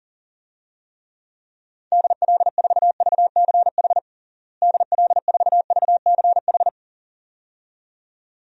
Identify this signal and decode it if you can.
Morse code, heard as a keyed tone, about 40 wpm, 700 Hz. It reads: DL4VCH DL4VCH
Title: DXCC-100 Call Signs - 6 Characters - Repeated 2x - 40wpm